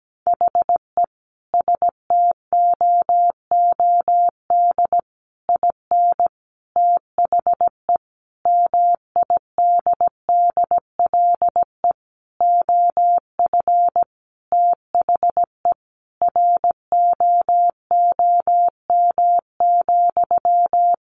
Morse code, heard as a keyed tone, about 17 wpm, 700 Hz. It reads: HE STOOD IN THE MIDDLE OF THE ROOM,
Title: Sentences from Top 1400 Words 17wpm